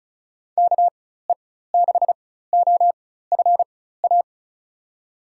Morse code, heard as a keyed tone, about 36 wpm, spaced as for 16 wpm, 700 Hz. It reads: KE6OFA
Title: USA Call Signs - 6 Characters - 4x Character Spacing - 35wpm